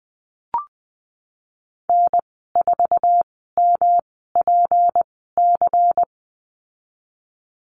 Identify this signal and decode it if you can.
Morse code, heard as a keyed tone, about 20 wpm, spaced as for 15 wpm, 700 Hz. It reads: N4MPC
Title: 179 Lesson 29g - Slash - Call Signs 20wpm